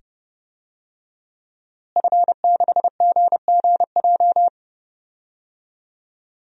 Morse code, heard as a keyed tone, about 31 wpm, 700 Hz. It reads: F6GGJ
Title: DXCC-100 Call Signs - 5 Characters - Repeated 2x - 30wpm